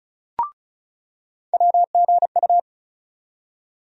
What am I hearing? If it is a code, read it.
Morse code, heard as a keyed tone, about 36 wpm, 700 Hz. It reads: WGU
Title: USA Call Sign Postfixes - 3 Letters 35wpm